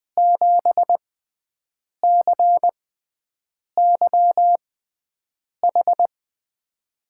Morse code, heard as a keyed tone, about 20 wpm, spaced as for 7 wpm, 700 Hz. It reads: 7CYH
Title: USA Call Signs - 5 Characters - 6x Character Spacing - 20wpm